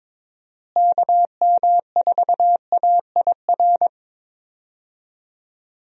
Morse code, heard as a keyed tone, about 23 wpm, 700 Hz. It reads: KM4AIR
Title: USA Call Signs - 6 Characters - Repeated 2x - 22wpm